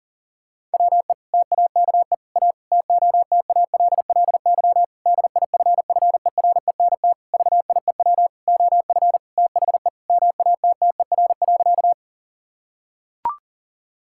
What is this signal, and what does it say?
Morse code, heard as a keyed tone, about 40 wpm, 700 Hz. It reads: WE TAKE A TOTALLY DIFFERENT VIEW OF THE MATTER.
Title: Sentences from Top 1600 Words 40wpm